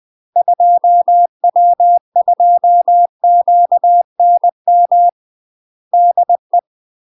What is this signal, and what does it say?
Morse code, heard as a keyed tone, about 20 wpm, 700 Hz. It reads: 2W2QNM DE
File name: UK MASTER QSOs TRAINING / 20WPM / UK MASTER QSOs 0052